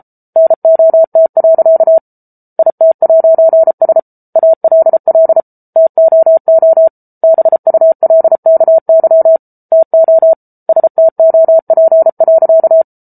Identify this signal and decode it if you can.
Morse code, heard as a keyed tone, about 34 wpm, 650 Hz. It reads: NOT. IT'S ALL TOO BULKY TO STOP.